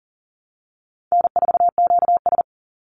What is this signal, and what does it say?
Morse code, heard as a keyed tone, about 40 wpm, 700 Hz. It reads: N4QS
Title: Easy Call Signs Repeated 2x - 40wpm